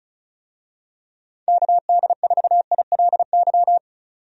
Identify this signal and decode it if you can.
Morse code, heard as a keyed tone, about 36 wpm, 700 Hz. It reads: KD4ILY
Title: USA Call Signs - 6 Characters - 35wpm